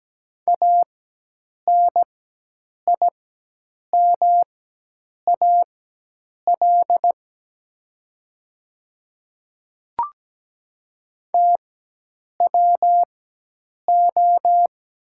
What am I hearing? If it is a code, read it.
Morse code, heard as a keyed tone, about 17 wpm, spaced as for 8 wpm, 700 Hz. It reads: ANIMAL TWO
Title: 4x Character Spacing - Top 200 Words - 17wpm